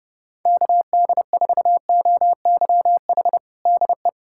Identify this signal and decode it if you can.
Morse code, heard as a keyed tone, about 30 wpm, 700 Hz. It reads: KD4OYH DE
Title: USA Callsign DE Callsign - 6 Characters - 30wpm